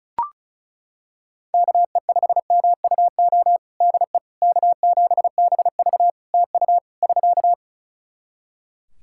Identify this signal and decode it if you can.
Morse code, heard as a keyed tone, about 36 wpm, 700 Hz. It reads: KE5MUO DE K7BV TU <SK>
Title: K1USN SST QSO Practice 35wpm